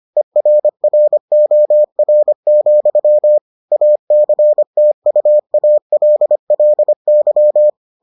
Morse code, heard as a keyed tone, about 25 wpm, 600 Hz. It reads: ERROR, ACTUALLY